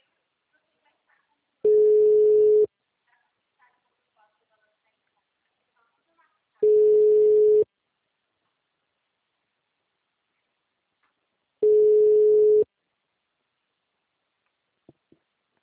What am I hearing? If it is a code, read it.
Ringback tone (Europe, 425 Hz), with speech.
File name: out- -2013- - - .400